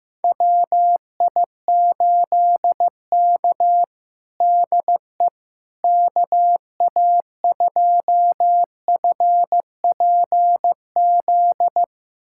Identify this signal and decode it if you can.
Morse code, heard as a keyed tone, about 16 wpm, 700 Hz. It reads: WI8K DE KA2FPZ
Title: USA Callsign DE Callsign 15wpm